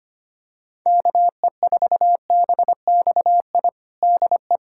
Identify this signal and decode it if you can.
Morse code, heard as a keyed tone, about 25 wpm, 700 Hz. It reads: KE4BXI DE
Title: USA Callsign DE Callsign 25wpm